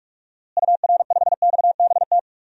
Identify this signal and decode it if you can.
Morse code, heard as a keyed tone, about 41 wpm, 700 Hz. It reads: UR5XBT
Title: DXCC-100 Call Signs - 6 Characters - Repeated 3x - 45wpm